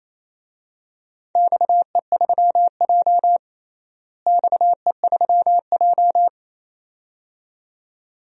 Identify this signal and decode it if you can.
Morse code, heard as a keyed tone, about 28 wpm, 700 Hz. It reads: XE3J XE3J
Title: DXCC-100 Call Signs - 4 Characters - Repeated 2x - 28wpm